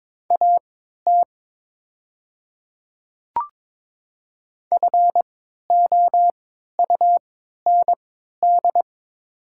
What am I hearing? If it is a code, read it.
Morse code, heard as a keyed tone, about 22 wpm, spaced as for 13 wpm, 690 Hz. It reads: AT FOUND
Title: 3x Character Spacing - Top 300 Words - 22wpm